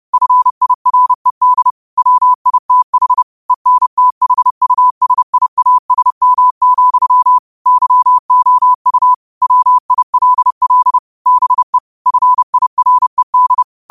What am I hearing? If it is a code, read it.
Morse code, heard as a keyed tone, about 30 wpm, 1000 Hz. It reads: FIRED WITH ENTHUSIASM, YOU WILL BE FIRED